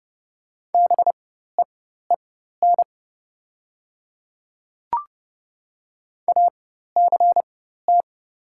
Morse code, heard as a keyed tone, about 30 wpm, spaced as for 14 wpm, 690 Hz. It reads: BEEN ACT
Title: 4x Character Spacing - Top 300 Words - 30wpm